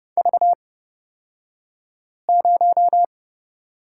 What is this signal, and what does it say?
Morse code, heard as a keyed tone, about 30 wpm, spaced as for 5 wpm, 700 Hz. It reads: V0
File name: Lesson 36d - BT - Two Characters 30wpm